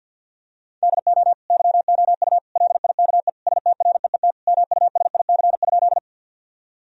Morse code, heard as a keyed tone, about 50 wpm, 700 Hz. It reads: DO YOU LIKE STREET MUSIC?
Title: Sentences from Top 1000 Words - Review Part 3 - 50wpm